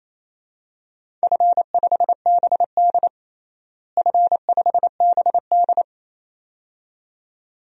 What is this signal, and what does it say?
Morse code, heard as a keyed tone, about 28 wpm, 700 Hz. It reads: F5BD F5BD
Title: DXCC-100 Call Signs - 4 Characters - Repeated 2x - 28wpm